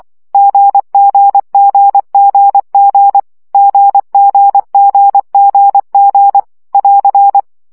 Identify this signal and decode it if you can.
Morse code, heard as a keyed tone, about 24 wpm, 800 Hz. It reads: GGGGG GGGGG <AR>
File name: CW gyakorlas g 20 baud